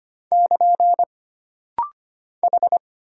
Morse code, heard as a keyed tone, about 25 wpm, 700 Hz. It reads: <KN> H
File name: Single Character (Letter:Number:Punctuation:Prosign) - Rapid-Fire - 25wpm